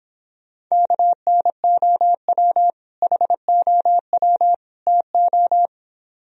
Morse code, heard as a keyed tone, about 26 wpm, 700 Hz. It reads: KNOW HOW TO